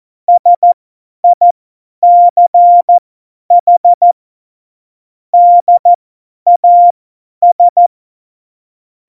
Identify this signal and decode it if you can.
Morse code, heard as a keyed tone, about 14 wpm, spaced as for 10 wpm, 700 Hz. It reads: SICH DAS
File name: Morsekurs / Witze / Witz 09. Januar 2018, 50 F70